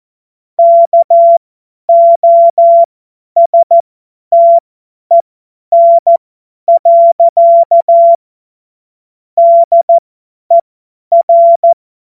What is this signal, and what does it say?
Morse code, heard as a keyed tone, about 14 wpm, spaced as for 10 wpm, 680 Hz. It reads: KOSTEN. DER
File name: Morsekurs / Witze / Witz 08. August 2023 50 F70